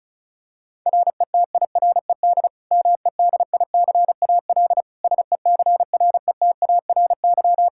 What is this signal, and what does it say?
Morse code, heard as a keyed tone, about 36 wpm, 700 Hz. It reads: RETIRED MEDICAL SECRETARY
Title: K1USN SST QSO Practice 35wpm